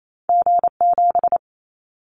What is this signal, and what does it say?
Morse code, heard as a keyed tone, about 28 wpm, 700 Hz. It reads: G7
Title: Callsign Prefixes for Top 25 DXCC - 28wpm